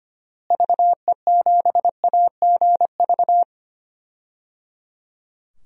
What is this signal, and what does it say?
Morse code, heard as a keyed tone, about 25 wpm, 700 Hz. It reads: VE7AGV
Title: DXCC-100 Call Signs - 6 Characters - Repeated 3x - 25wpm